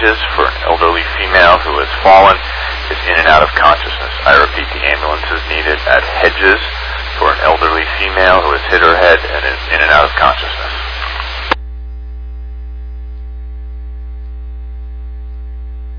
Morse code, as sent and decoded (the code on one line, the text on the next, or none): none